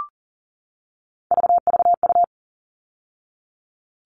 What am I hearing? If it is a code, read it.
Morse code, heard as a keyed tone, about 40 wpm, 700 Hz. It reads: VVU
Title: All Cut Numbers - 3-Digit 40wpm